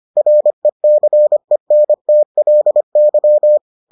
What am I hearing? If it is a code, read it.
Morse code, heard as a keyed tone, about 25 wpm, 600 Hz. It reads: RECENTLY